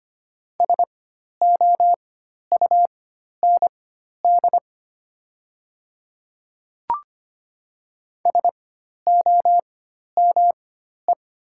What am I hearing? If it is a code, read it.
Morse code, heard as a keyed tone, about 25 wpm, spaced as for 12 wpm, 700 Hz. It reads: SOUND SOME
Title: 4x Character Spacing - Top 100 Words - 25wpm